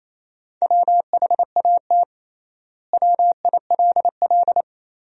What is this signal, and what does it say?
Morse code, heard as a keyed tone, about 28 wpm, 700 Hz. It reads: WHAT WILL
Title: 3x Word-Spacing - Sentences from Top 400 Words - 28wpm